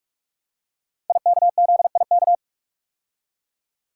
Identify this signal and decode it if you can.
Morse code, heard as a keyed tone, about 44 wpm, 700 Hz. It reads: IKZIK
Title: Code Groups - 5 Letters - 45wpm